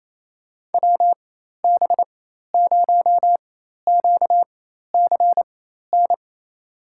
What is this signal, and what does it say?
Morse code, heard as a keyed tone, about 28 wpm, spaced as for 13 wpm, 700 Hz. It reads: WB0QCN
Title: USA Call Signs - 6 Characters - 4x Character Spacing - 28wpm